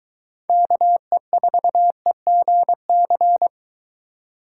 Morse code, heard as a keyed tone, about 22 wpm, 700 Hz. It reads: KE4EGC